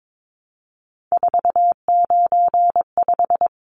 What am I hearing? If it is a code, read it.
Morse code, heard as a keyed tone, about 22 wpm, 700 Hz. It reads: 495
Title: Numbers - 3-Digit 22wpm